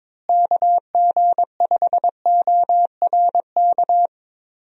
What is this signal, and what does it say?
Morse code, heard as a keyed tone, about 22 wpm, 700 Hz. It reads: KG5ORK
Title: USA Call Signs - 6 Characters - Repeated 2x - 22wpm